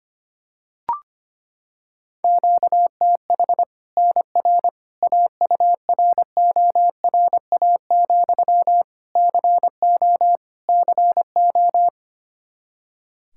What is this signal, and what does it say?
Morse code, heard as a keyed tone, about 25 wpm, 700 Hz. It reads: QTH NR AURORA, CO CO
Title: Practice QSO - Exchange 1 - Beginning - 25wpm